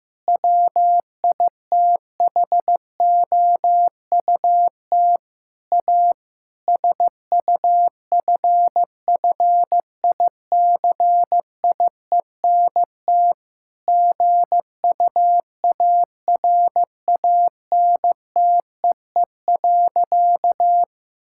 Morse code, heard as a keyed tone, about 15 wpm, 700 Hz. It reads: WITHOUT A SUFFICIENT GUARANTEE.
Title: Sentences from Top 1600 Words 15wpm